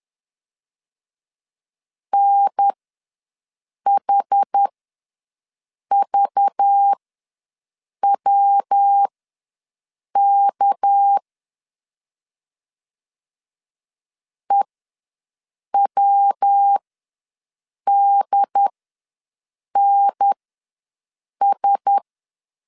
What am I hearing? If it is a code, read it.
Morse code, heard as a keyed tone, about 10 wpm, spaced as for 6 wpm, 800 Hz. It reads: NHVWK EWDNS
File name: LEZIONI / 6 PKX0 / 6 PKX0 ws 10wpm 4220